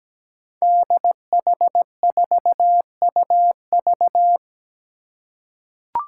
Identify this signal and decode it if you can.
Morse code, heard as a keyed tone, about 17 wpm, 700 Hz. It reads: DH4UV
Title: Five Challenging Letter-Number Combinations 17wpm